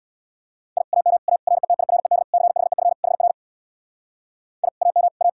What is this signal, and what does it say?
Morse code, heard as a keyed tone, about 65 wpm, 700 Hz. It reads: I AM A RETIRED CHEF HR I AM A